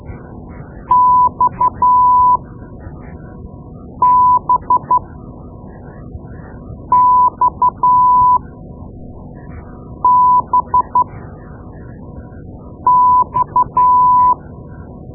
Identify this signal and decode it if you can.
Morse code, heard as a keyed tone, about 12 wpm, spaced as for 4 wpm, 1000 Hz. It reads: XBXBX
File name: Morse Lessons / Morse Lessons / Tape-2B